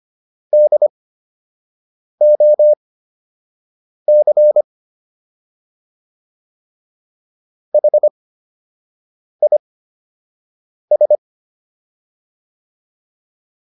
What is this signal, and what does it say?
Morse code, heard as a keyed tone, about 25 wpm, spaced as for 6 wpm, 600 Hz. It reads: DOC HIS